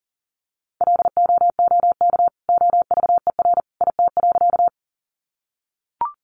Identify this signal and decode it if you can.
Morse code, heard as a keyed tone, about 40 wpm, 700 Hz. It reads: LOOK OVER IT.
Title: Sentences from Top 100 Words 40wpm